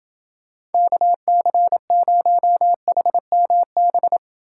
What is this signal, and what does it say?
Morse code, heard as a keyed tone, about 29 wpm, 700 Hz. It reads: KC0HMB